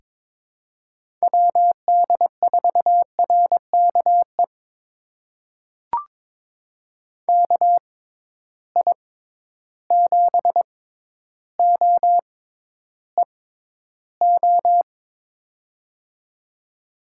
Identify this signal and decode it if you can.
Morse code, heard as a keyed tone, about 22 wpm, spaced as for 11 wpm, 700 Hz. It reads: WD4RKE KI7OEO
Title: USA Call Signs - 6 Characters - 6x Character Spacing - 22wpm